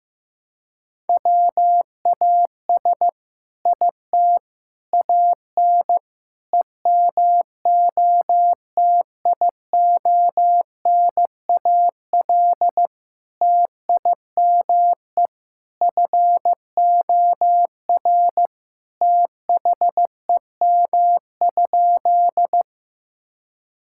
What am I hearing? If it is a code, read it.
Morse code, heard as a keyed tone, about 15 wpm, 700 Hz. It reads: WAS IT AN EMOTIONAL TIME FOR THEM?
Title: Sentences from Top 1000 Words - Review Part 5 - 15wpm